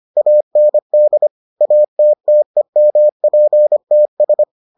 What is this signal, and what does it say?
Morse code, heard as a keyed tone, about 25 wpm, 600 Hz. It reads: AND ATTEMPTS